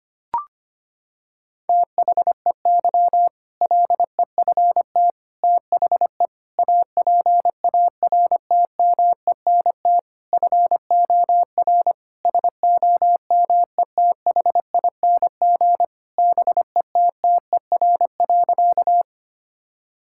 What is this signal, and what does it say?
Morse code, heard as a keyed tone, about 25 wpm, 700 Hz. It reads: THEY LEFT THE APARTMENT FOR SOMETHING BETTER.
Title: Sentences from Top 1000 Words - Review Part 6 - 25wpm